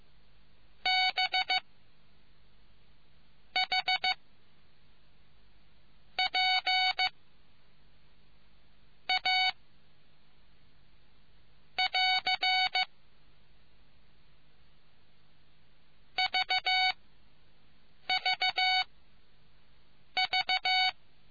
Morse code, heard as a keyed tone, about 15 wpm, spaced as for 6 wpm, 750 Hz. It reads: B H P A <AR> VVV